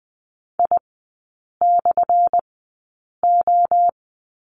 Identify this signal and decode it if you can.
Morse code, heard as a keyed tone, about 20 wpm, spaced as for 8 wpm, 700 Hz. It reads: I/O
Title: 184 Lesson 30e - Letter b - Three Characters 20wpm